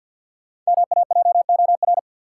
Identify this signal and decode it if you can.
Morse code, heard as a keyed tone, about 51 wpm, 700 Hz. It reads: MAJOR
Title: Five Letter Words - 50wpm